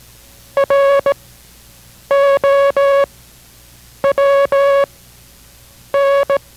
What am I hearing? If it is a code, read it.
Morse code, heard as a keyed tone, about 17 wpm, spaced as for 7 wpm, 560 Hz. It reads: ROWN